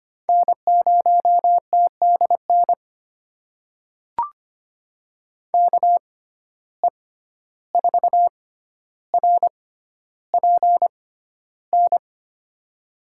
Morse code, heard as a keyed tone, about 25 wpm, spaced as for 12 wpm, 700 Hz. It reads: N0TDN KE4RPN
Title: USA Call Signs - 6x Character Spacing - 25wpm